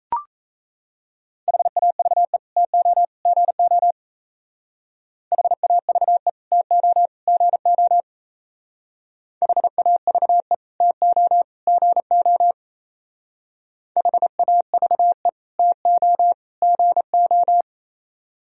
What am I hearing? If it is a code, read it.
Morse code, heard as a keyed tone, about 35 wpm, 700 Hz. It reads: HAVE TO GO HAVE TO GO HAVE TO GO HAVE TO GO
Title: Speed-Racing - Sets of 3 Words 28wpm